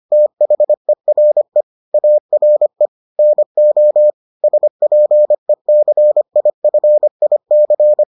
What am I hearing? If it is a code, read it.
Morse code, heard as a keyed tone, about 25 wpm, 600 Hz. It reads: THERE ARE NO SPECIFIC